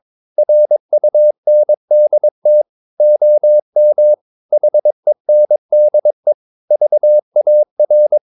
Morse code, heard as a keyed tone, about 22 wpm, 600 Hz. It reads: RUNDT OM HENDE VAR
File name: Ferie QRS 22wpm